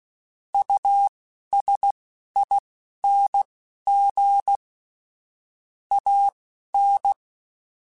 Morse code, heard as a keyed tone, about 16 wpm, spaced as for 11 wpm, 800 Hz. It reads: USING AN
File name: QSOs / QSOs-mp3s / WD0FBT 1@15